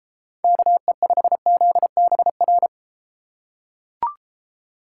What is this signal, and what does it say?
Morse code, heard as a keyed tone, about 32 wpm, 700 Hz. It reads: KE5ZBR